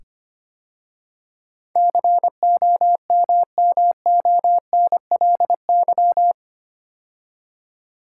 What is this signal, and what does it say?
Morse code, heard as a keyed tone, about 25 wpm, 700 Hz. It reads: COMMONLY
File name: 8 Letter Words - First 3 Letters Given - 25wpm